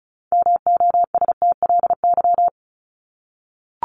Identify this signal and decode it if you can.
Morse code, heard as a keyed tone, about 34 wpm, 700 Hz. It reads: MOSTLY
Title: Top 1001-1250 Words 35wpm